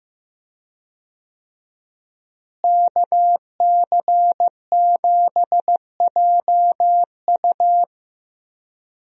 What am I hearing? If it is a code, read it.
Morse code, heard as a keyed tone, about 15 wpm, 700 Hz. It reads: KC7JU
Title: USA Call Signs - 5 Characters - Repeated 2x - 15wpm